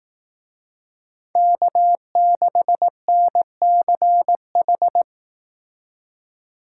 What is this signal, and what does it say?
Morse code, heard as a keyed tone, about 19 wpm, 700 Hz. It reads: K6NCH
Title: Speed Racing - USA Call Signs 15wpm